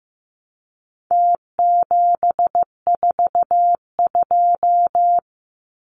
Morse code, heard as a keyed tone, about 15 wpm, 700 Hz. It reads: T742
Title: Numbers - Contest Style - 4-Digit 15wpm